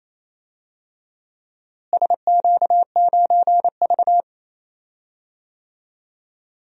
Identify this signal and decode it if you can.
Morse code, heard as a keyed tone, about 28 wpm, 700 Hz. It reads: SQ9V
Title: DXCC-100 Call Signs - 4 Characters - Repeated 2x - 28wpm